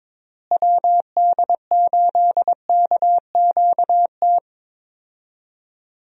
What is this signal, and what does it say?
Morse code, heard as a keyed tone, about 22 wpm, 700 Hz. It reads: WD8KQT
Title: USA Call Signs - 6 Characters - Repeated 3x - 22wpm